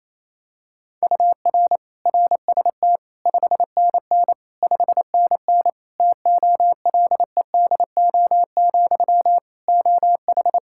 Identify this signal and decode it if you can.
Morse code, heard as a keyed tone, about 28 wpm, 700 Hz. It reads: UR RST 5NN 5NN TOLEDO, OH
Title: Practice QSO - Exchange 1 - Beginning - 28wpm